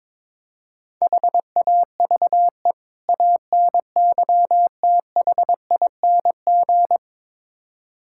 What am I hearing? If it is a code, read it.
Morse code, heard as a keyed tone, about 22 wpm, 700 Hz. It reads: HAVE ANYTHING
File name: practice / 22wpm / Build A Phrase - 5-Word Phrases - 22wpm